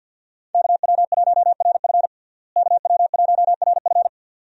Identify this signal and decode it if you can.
Morse code, heard as a keyed tone, about 46 wpm, 700 Hz. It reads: KW1RF KW1RF
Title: USA Call Signs - 5 Characters - Repeated 2x - 50wpm